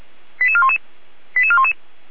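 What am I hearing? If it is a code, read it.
Five-tone selective call (ZVEI), heard as quick runs of five tones, tones of about 70 ms. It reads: 88411 88411
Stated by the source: ZVEI five-tone code 88335 received by SDR, from the feuerwehr-orlamuende.org (fire brigade) audio server.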